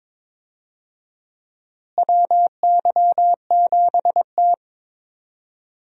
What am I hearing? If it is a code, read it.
Morse code, heard as a keyed tone, about 22 wpm, 700 Hz. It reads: WY7T
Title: USA Call Signs - 4 Characters - Repeated 2x - 22wpm